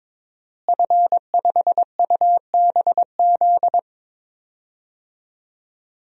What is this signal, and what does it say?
Morse code, heard as a keyed tone, about 22 wpm, 700 Hz. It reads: F5UBZ
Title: DXCC-100 Call Signs - 5 Characters - Repeated 3x - 22wpm